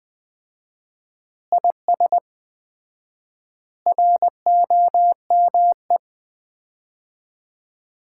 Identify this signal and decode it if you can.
Morse code, heard as a keyed tone, about 20 wpm, 700 Hz. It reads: IS ROME
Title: Common QSO Phrases - 4x Word-Spacing - 20wpm